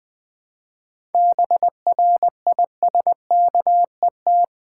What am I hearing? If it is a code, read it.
Morse code, heard as a keyed tone, about 20 wpm, 690 Hz. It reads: BRISKET
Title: Difficult Words with Hints - 20wpm